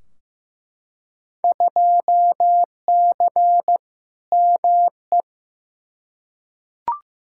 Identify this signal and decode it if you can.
Morse code, heard as a keyed tone, about 15 wpm, 700 Hz. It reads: 2C ME
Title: Field Day Exchanges Encore - 15wpm